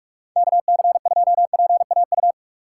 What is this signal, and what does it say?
Morse code, heard as a keyed tone, about 42 wpm, 690 Hz. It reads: KC2PAU
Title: USA Call Signs - 6 Characters - 3x Character Spacing - 45wpm